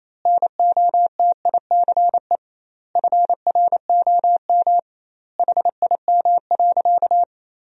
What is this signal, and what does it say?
Morse code, heard as a keyed tone, about 28 wpm, 700 Hz. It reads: NOTICE FROM HIM.
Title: 2x Word-Spacing - Sentences from Top 400 Words - 28wpm